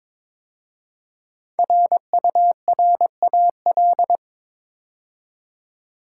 Morse code, heard as a keyed tone, about 23 wpm, 700 Hz. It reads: RURAL
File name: 5 Letter Words - First 3 Letters Given - 22wpm